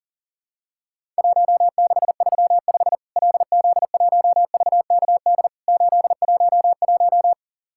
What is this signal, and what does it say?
Morse code, heard as a keyed tone, about 40 wpm, 700 Hz. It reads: 1635 LZ1VKD 811
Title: Worked All Europe DX Contest - QTC Exchange - 40wpm